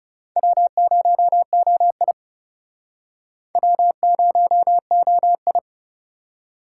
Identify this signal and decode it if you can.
Morse code, heard as a keyed tone, about 41 wpm, 700 Hz. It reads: W0OI W0OI